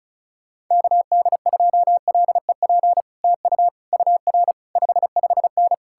Morse code, heard as a keyed tone, about 47 wpm, spaced as for 41 wpm, 700 Hz. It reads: KD2LEP TU UR 55N
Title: Speed-Racing Common QSO Phrases Encore - 35wpm